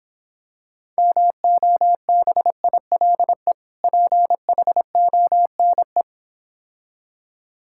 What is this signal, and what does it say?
Morse code, heard as a keyed tone, about 26 wpm, 700 Hz. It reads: MOBILE PHONE